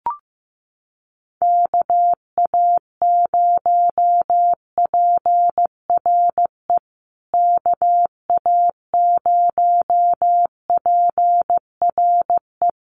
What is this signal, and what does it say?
Morse code, heard as a keyed tone, about 15 wpm, 700 Hz. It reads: KA0PRE KA0PRE
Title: USA Call Signs Repeated 3x 15wpm